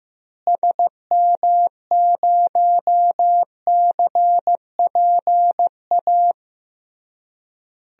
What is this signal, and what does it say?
Morse code, heard as a keyed tone, about 15 wpm, 700 Hz. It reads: SM0CPA
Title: DXCC-100 Call Signs - 6 Characters - Repeated 3x - 15wpm